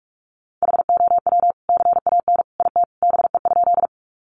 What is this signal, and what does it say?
Morse code, heard as a keyed tone, about 44 wpm, 700 Hz. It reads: HOW CAN IT BE?
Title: Sentences from Top 100 Words 45wpm